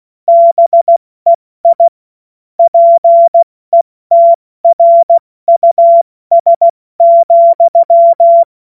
Morse code, heard as a keyed tone, about 16 wpm, 680 Hz. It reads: BEI PETRUS,
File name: Morsekurs / Witze / Witz 25. Oktober 2022 70 F80